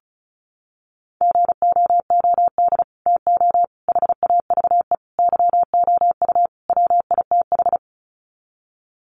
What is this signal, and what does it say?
Morse code, heard as a keyed tone, about 34 wpm, 700 Hz. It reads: GOOD TO HAVE YOU WITH
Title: Sets of 5 Words Encore2 35wpm